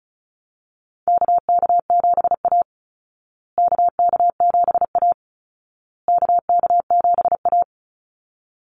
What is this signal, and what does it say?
Morse code, heard as a keyed tone, about 34 wpm, 700 Hz. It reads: KK7A KK7A KK7A
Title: Call Signs Repeated 2x - 35wpm